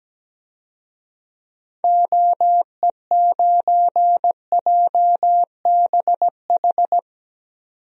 Morse code, heard as a keyed tone, about 17 wpm, 700 Hz. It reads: OE9JBH
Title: DXCC-100 Call Signs - 6 Characters - 17wpm